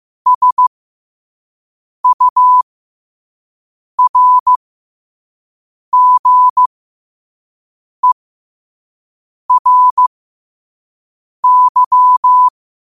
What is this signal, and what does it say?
Morse code, heard as a keyed tone, about 15 wpm, spaced as for 5 wpm, 1000 Hz. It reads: SURGERY
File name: qotdfiles / 5wpm